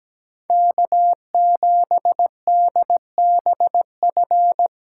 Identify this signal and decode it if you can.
Morse code, heard as a keyed tone, about 20 wpm, 700 Hz. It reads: K7DBF
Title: Speed Racing - USA 1x3 Call Signs 17wpm